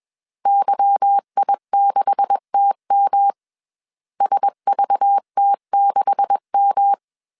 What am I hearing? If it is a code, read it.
Morse code, heard as a keyed tone, about 21 wpm, 800 Hz. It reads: YI6TM SVT6M
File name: CW / DET / 1 9 1500C S C 20wpm 2310